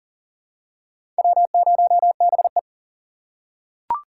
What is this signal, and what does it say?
Morse code, heard as a keyed tone, about 39 wpm, 700 Hz. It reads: W0BE
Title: USA Call Signs - 4x Character Spacing - 40wpm